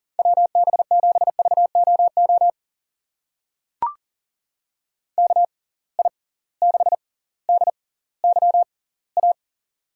Morse code, heard as a keyed tone, about 40 wpm, spaced as for 20 wpm, 700 Hz. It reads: WB7VOO KI6DYA
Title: USA Call Signs - 6 Characters - 6x Character Spacing - 40wpm